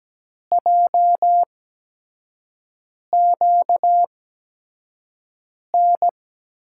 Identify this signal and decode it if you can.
Morse code, heard as a keyed tone, about 17 wpm, spaced as for 5 wpm, 700 Hz. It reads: JQN
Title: USA Call Signs - 8x Character Spacing - 17wpm